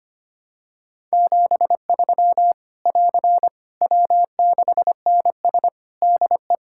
Morse code, heard as a keyed tone, about 25 wpm, 700 Hz. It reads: 73 <AR> W6NS DE
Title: Practice QSO - Exchange 4 - Ending - 25wpm